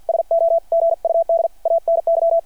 Morse code, heard as a keyed tone, about 51 wpm, 660 Hz. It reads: SO GUD ANY
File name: archived / CRSnet